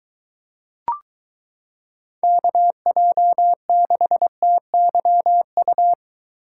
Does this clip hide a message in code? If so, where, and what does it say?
Yes: Morse code from 2.2 to 5.9 s, heard as a keyed tone, about 22 wpm, 700 Hz. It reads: KJ6TYU